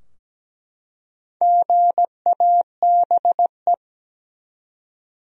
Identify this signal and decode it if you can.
Morse code, heard as a keyed tone, about 17 wpm, 700 Hz. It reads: GABE
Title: USA Common Names - 17wpm